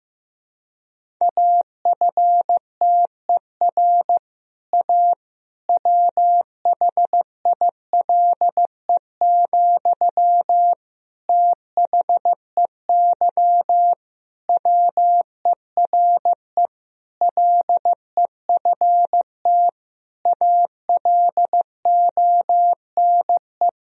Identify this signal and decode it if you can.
Morse code, heard as a keyed tone, about 15 wpm, 700 Hz. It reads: AFTER A WHILE, THEY WERE LEFT ALONE
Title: Sentences from Top 1000 Words - Review Part 1 - 15wpm